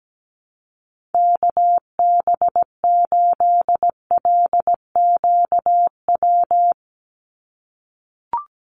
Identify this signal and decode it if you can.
Morse code, heard as a keyed tone, about 17 wpm, 700 Hz. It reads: KB8LQW
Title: USA Call Signs 17wpm